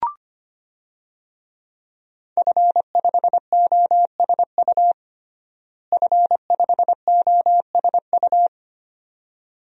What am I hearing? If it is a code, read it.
Morse code, heard as a keyed tone, about 25 wpm, 700 Hz. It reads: F5OSU F5OSU
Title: DXCC-100 Call Signs - 5 Characters - Repeated 2x - 25wpm